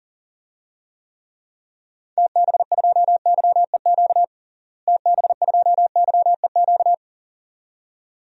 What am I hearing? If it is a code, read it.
Morse code, heard as a keyed tone, about 40 wpm, 700 Hz. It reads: TB2YEQ TB2YEQ
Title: DXCC-100 Call Signs - 6 Characters - Repeated 2x - 40wpm